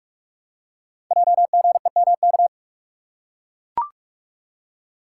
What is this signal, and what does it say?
Morse code, heard as a keyed tone, about 44 wpm, 690 Hz. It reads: JGEMK
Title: Code Groups - 5 Letters - 45wpm